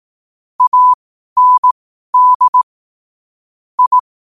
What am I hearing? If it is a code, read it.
Morse code, heard as a keyed tone, about 18 wpm, spaced as for 12 wpm, 1000 Hz. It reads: AND I
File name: qotdfiles / 12wpm